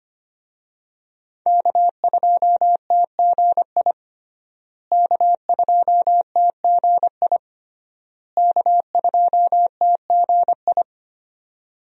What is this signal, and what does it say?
Morse code, heard as a keyed tone, about 25 wpm, 700 Hz. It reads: K2TGI K2TGI K2TGI
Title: USA Call Signs - 5 Characters - Repeated 3x - 25wpm